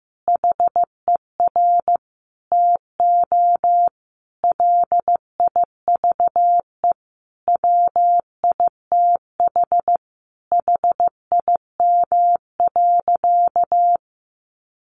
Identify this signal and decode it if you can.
Morse code, heard as a keyed tone, about 15 wpm, 700 Hz. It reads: HER TO LIVE WITH HIM.
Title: Sentences from Top 1100 Words 15wpm